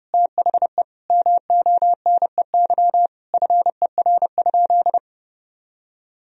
Morse code, heard as a keyed tone, about 30 wpm, 700 Hz. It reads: THE MONEY FER?
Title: Sentences from Top 400 Words and Abbreviations 30wpm